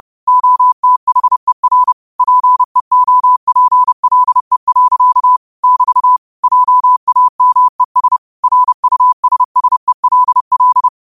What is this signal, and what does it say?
Morse code, heard as a keyed tone, about 30 wpm, 1000 Hz. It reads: OTHER PEOPLE. <BT> JAMES RUSSELL